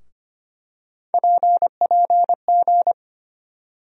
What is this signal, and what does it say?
Morse code, heard as a keyed tone, about 25 wpm, 700 Hz. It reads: PPG
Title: USA Call Sign Postfixes - 3 Letters 25wpm